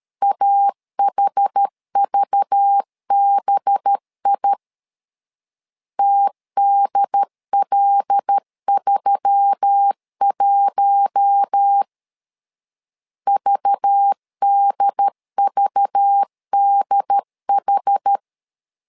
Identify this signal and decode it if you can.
Morse code, heard as a keyed tone, about 13 wpm, 800 Hz. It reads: AHVBI TDL31 VDVDH
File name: LEZIONI / 5 LBVC / 5 LBV 12wpm 1855